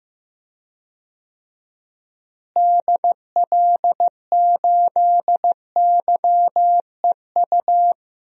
Morse code, heard as a keyed tone, about 15 wpm, 700 Hz. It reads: DL8YEU
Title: DXCC-100 Call Signs - 6 Characters - Repeated 2x - 15wpm